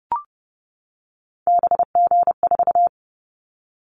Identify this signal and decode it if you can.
Morse code, heard as a keyed tone, about 30 wpm, 700 Hz. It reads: BG4
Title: Callsign Prefixes for Top 100 DXCC - 30wpm